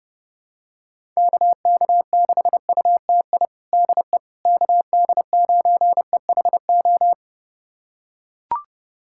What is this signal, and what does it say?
Morse code, heard as a keyed tone, about 32 wpm, 700 Hz. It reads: KK6UTI DE KD9EHO
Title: USA Callsign DE Callsign 30wpm